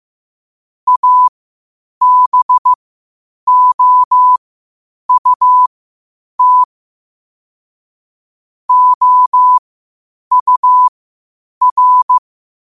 Morse code, heard as a keyed tone, about 15 wpm, spaced as for 8 wpm, 1000 Hz. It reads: ABOUT OUR
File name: qotdfiles / 8wpm